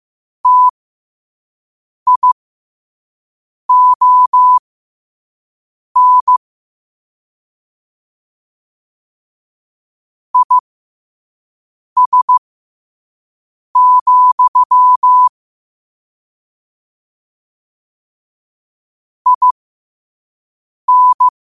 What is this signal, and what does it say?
Morse code, heard as a keyed tone, about 15 wpm, spaced as for 5 wpm, 1000 Hz. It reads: TION IS, IN
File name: qotdfiles / 5wpm